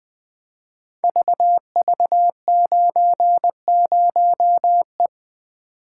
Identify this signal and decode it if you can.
Morse code, heard as a keyed tone, about 20 wpm, 700 Hz. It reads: VV90E
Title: Code Groups - 5 Letter:Numbers - 20wpm